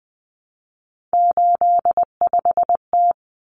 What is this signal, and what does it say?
Morse code, heard as a keyed tone, about 20 wpm, 700 Hz. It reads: 85T
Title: Common Cut Numbers - 3-Digit 20wpm